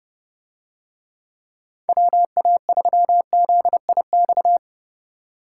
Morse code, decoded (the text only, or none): WA3ZIX